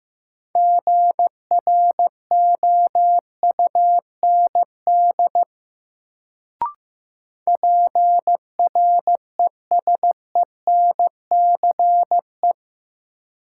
Morse code, heard as a keyed tone, about 15 wpm, 700 Hz. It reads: GROUND PRESENCE
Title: Sets of 3 Words - New Word Drill - 15wpm